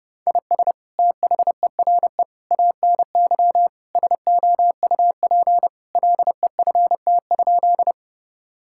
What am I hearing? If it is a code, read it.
Morse code, heard as a keyed tone, about 31 wpm, 700 Hz. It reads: IS THERE ANY SOUP LEFT?
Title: Sentences from Top 1000 Words - Review Part 6 - 30wpm